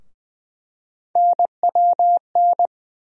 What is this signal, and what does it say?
Morse code, heard as a keyed tone, about 20 wpm, 700 Hz. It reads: NWN
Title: USA Call Sign Postfixes - 3 Letters 20wpm